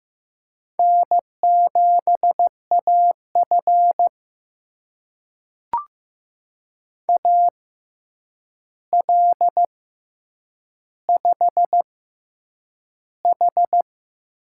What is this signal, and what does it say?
Morse code, heard as a keyed tone, about 15 wpm, spaced as for 8 wpm, 700 Hz. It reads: N7AF AL5H
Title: USA Call Signs - 4 Characters - 6x Character Spacing - 15wpm